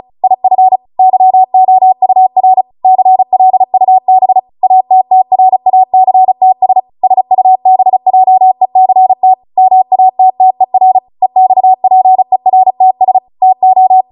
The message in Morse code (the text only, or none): IF YOUR CLUB ATTRACTS SUBJECT MATTER EXPERTS TO